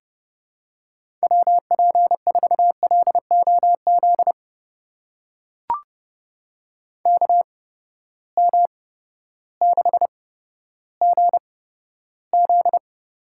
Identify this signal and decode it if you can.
Morse code, heard as a keyed tone, about 30 wpm, spaced as for 14 wpm, 700 Hz. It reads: WP4LOZ KM6GZ
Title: USA Call Signs - 6 Characters - 8x Character Spacing - 30wpm